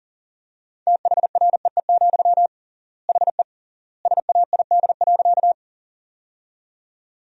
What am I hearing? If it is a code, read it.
Morse code, heard as a keyed tone, about 39 wpm, 700 Hz. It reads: THREE, HE SAID.
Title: 3x Word-Spacing - Sentences from Top 200 Words - 40wpm